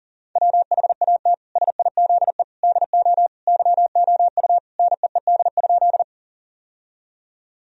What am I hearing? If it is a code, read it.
Morse code, heard as a keyed tone, about 40 wpm, 700 Hz. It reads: WHAT SIZE DO YOU NEED?
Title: Sentences from Top 1000 Words - Review Part 2 - 40wpm